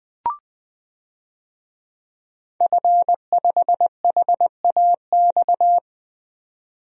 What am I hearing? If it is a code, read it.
Morse code, heard as a keyed tone, about 20 wpm, 700 Hz. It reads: F5HAX
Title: DXCC-100 Call Signs - 5 Characters - 20wpm